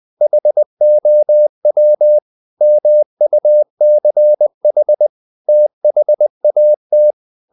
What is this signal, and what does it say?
Morse code, heard as a keyed tone, about 20 wpm, 600 Hz. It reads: HOW MUCH THAT
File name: the old man and the sea part 2 20wpm